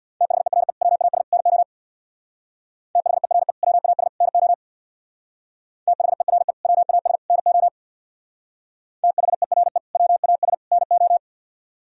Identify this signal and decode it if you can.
Morse code, heard as a keyed tone, about 60 wpm, 700 Hz. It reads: THERE WAS NO THERE WAS NO THERE WAS NO THERE WAS NO
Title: Speed-Racing - Sets of 3 Words 50wpm